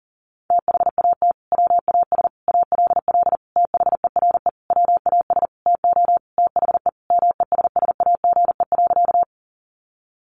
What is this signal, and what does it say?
Morse code, heard as a keyed tone, about 40 wpm, 700 Hz. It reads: THAT WAS ALL THERE WAS TO THE MESSAGE.
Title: Sentences from Top 1100 Words 40wpm